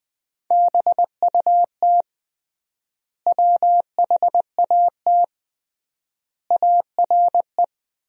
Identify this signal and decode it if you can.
Morse code, heard as a keyed tone, about 21 wpm, 700 Hz. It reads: BUT WHAT ARE
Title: 3x Word-Spacing - Sentences from Top 100 Words - 20wpm